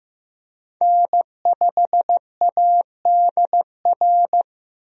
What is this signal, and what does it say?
Morse code, heard as a keyed tone, about 15 wpm, 700 Hz. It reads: N5ADR
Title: USA Call Signs - 5 Characters - 15wpm